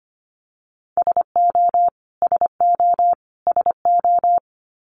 Morse code, heard as a keyed tone, about 25 wpm, 700 Hz. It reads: SO SO SO
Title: Top 300 Words Repeated 25wpm